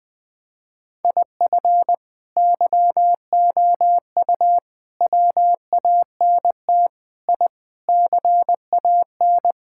Morse code, heard as a keyed tone, about 20 wpm, 700 Hz. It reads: IF YOU WANT I CAN
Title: Sentences with 3-Word Phrases - Spoken Beginning - 20wpm